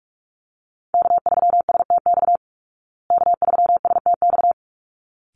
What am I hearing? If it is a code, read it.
Morse code, heard as a keyed tone, about 44 wpm, 700 Hz. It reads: K3STX K3STX
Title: Call Signs Repeated 3x - 45wpm